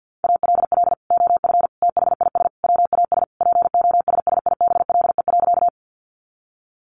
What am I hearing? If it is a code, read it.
Morse code, heard as a keyed tone, about 50 wpm, 700 Hz. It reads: ALL OF THIS WAS POSSIBLE.
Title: Sentences from Top 500 Words 50wpm